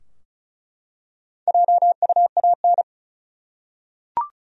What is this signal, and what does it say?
Morse code, heard as a keyed tone, about 35 wpm, 700 Hz. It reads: JUAN
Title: USA Common Names - Max 5 Letters - 35wpm